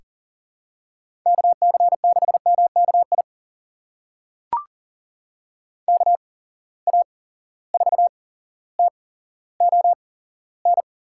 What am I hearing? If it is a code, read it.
Morse code, heard as a keyed tone, about 39 wpm, spaced as for 17 wpm, 700 Hz. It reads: KC6MKI KA4TON
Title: USA Call Signs - 6 Characters - 8x Character Spacing - 40wpm